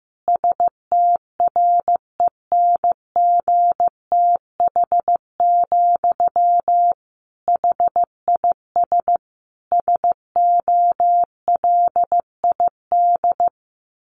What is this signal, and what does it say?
Morse code, heard as a keyed tone, about 15 wpm, 700 Hz. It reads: STRENGTH, HIS SOLID
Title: Sentences from Top 1500 Words 15wpm